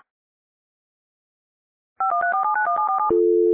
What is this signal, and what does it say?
DTMF, heard as telephone touch tones, tones of about 120 ms. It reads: 5134791747
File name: radiocap / REC z